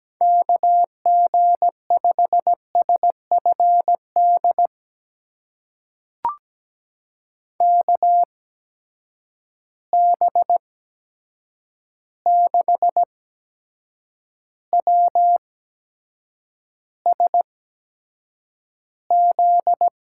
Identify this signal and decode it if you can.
Morse code, heard as a keyed tone, about 17 wpm, spaced as for 7 wpm, 700 Hz. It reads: KG5SFD KB6WSZ
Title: USA Call Signs - 6 Characters - 8x Character Spacing - 17wpm